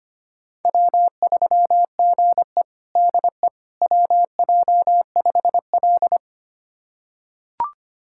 Morse code, heard as a keyed tone, about 25 wpm, 700 Hz. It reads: W3GE DE WJ5L
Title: USA Callsign DE Callsign - 4 Characters - 25wpm